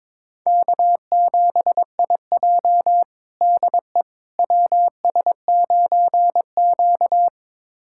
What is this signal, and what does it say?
Morse code, heard as a keyed tone, about 22 wpm, 700 Hz. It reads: K7IJ DE WS9Q
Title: USA Callsign DE Callsign - 4 Characters - 22wpm